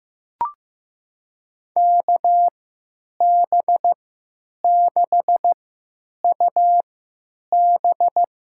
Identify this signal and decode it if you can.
Morse code, heard as a keyed tone, about 15 wpm, spaced as for 9 wpm, 700 Hz. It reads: KB6UB
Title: USA Call Signs - 6 Characters - 3x Character Spacing - 15wpm